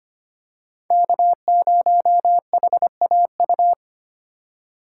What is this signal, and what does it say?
Morse code, heard as a keyed tone, about 25 wpm, 700 Hz. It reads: K0HAU
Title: USA Call Signs - 3x Character Spacing - 25wpm